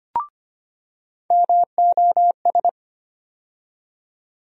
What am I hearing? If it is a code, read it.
Morse code, heard as a keyed tone, about 25 wpm, 700 Hz. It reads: MOS
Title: USA Call Sign Postfixes - 3 Letters 25wpm